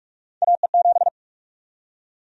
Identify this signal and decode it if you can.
Morse code, heard as a keyed tone, about 46 wpm, 700 Hz. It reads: AE7
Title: USA Call Sign Prefixes - 45wpm